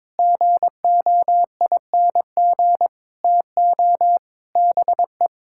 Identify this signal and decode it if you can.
Morse code, heard as a keyed tone, about 22 wpm, 700 Hz. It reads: GOING TO BE
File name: practice / 22wpm / Sets of 7 Words - 22wpm